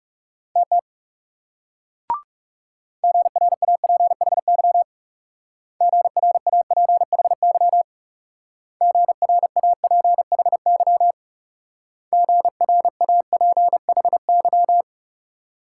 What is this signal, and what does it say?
Morse code, heard as a keyed tone, about 41 wpm, 700 Hz. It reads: TT GRAPHY GRAPHY GRAPHY GRAPHY